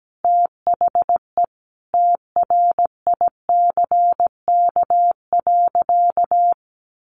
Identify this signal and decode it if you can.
Morse code, heard as a keyed tone, about 17 wpm, 700 Hz. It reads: THE TRICK.
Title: Sentences from Top 1500 Words 17wpm